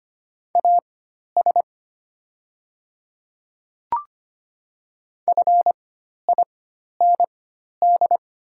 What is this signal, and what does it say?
Morse code, heard as a keyed tone, about 25 wpm, spaced as for 12 wpm, 700 Hz. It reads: AS FIND
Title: 4x Character Spacing - Top 200 Words - 25wpm